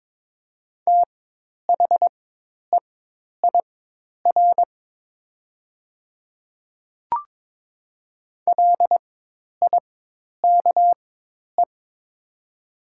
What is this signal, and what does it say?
Morse code, heard as a keyed tone, about 22 wpm, spaced as for 10 wpm, 700 Hz. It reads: THEIR LIKE
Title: 4x Character Spacing - Top 100 Words - 22wpm